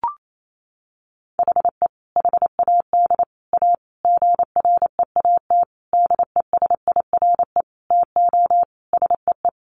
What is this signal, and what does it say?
Morse code, heard as a keyed tone, about 28 wpm, 700 Hz. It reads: HE HAD A GREAT DESIRE TO SEE
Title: Sentences from Top 1100 Words 28wpm